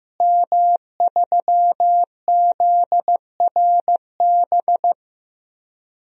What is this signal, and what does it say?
Morse code, heard as a keyed tone, about 16 wpm, 700 Hz. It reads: M3ZRB
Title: DXCC-100 Call Signs - 5 Characters - Repeated 3x - 15wpm